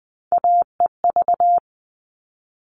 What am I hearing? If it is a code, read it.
Morse code, heard as a keyed tone, about 20 wpm, 700 Hz. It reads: AEV
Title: All Cut Numbers - 3-Digit 20wpm